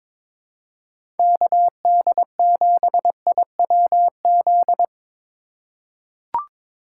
Morse code, heard as a keyed tone, about 21 wpm, 700 Hz. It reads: KD7IWZ
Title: USA Call Signs - 6 Characters - 3x Character Spacing - 22wpm